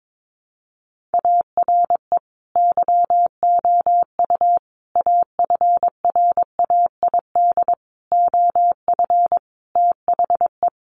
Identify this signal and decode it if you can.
Morse code, heard as a keyed tone, about 22 wpm, 700 Hz. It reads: ARE YOU AFRAID OF THE
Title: Sentences from Top 1500 Words 22wpm